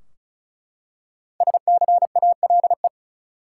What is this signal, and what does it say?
Morse code, heard as a keyed tone, about 36 wpm, 700 Hz. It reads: SCALE
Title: Five Letter Words - 35wpm